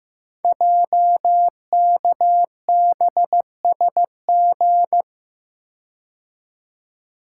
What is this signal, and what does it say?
Morse code, heard as a keyed tone, about 15 wpm, 700 Hz. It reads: JKBSG
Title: Code Groups - 5 Letters - 15wpm